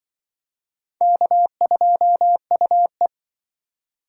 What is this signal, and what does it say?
Morse code, heard as a keyed tone, about 25 wpm, 700 Hz. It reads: K2UE
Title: Speed Racing - USA 1x2 Call Signs 20wpm